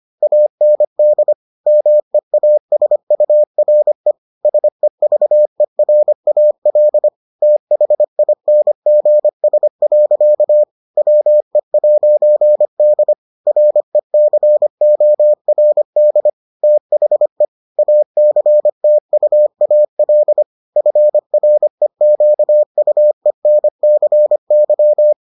Morse code, heard as a keyed tone, about 25 wpm, 600 Hz. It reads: AND MEASURE SEVERAL THINGS. WE'D RECORD THE ACTUAL FREQUENCY